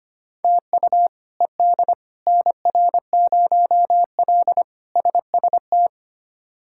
Morse code, heard as a keyed tone, about 25 wpm, 700 Hz. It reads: TU ED NR0L SST
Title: K1USN SST QSO Practice - 25wpm